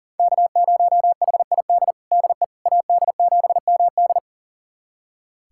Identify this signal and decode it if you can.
Morse code, heard as a keyed tone, about 40 wpm, 700 Hz. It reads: K0HID DE AD7MD
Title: USA Callsign DE Callsign - 5 Characters - 40wpm